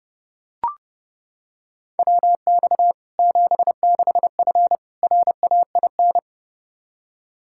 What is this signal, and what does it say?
Morse code, heard as a keyed tone, about 32 wpm, 700 Hz. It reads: WX 76F RAIN
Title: Practice QSO - Exchange 2 - Info - 30wpm